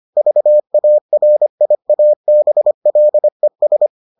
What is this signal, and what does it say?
Morse code, heard as a keyed tone, about 25 wpm, 600 Hz. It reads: VARIABLES